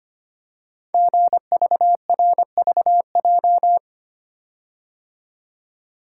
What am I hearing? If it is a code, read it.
Morse code, heard as a keyed tone, about 25 wpm, 700 Hz. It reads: GVRVJ
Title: Code Groups - 5 Letter:Numbers - 25wpm